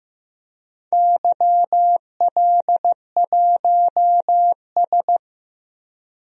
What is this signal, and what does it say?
Morse code, heard as a keyed tone, about 16 wpm, 700 Hz. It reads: YL1S
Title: DXCC-100 Call Signs - 4 Characters - 15wpm